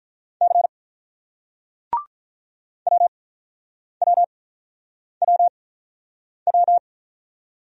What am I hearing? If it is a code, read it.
Morse code, heard as a keyed tone, about 47 wpm, 700 Hz. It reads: C W W W W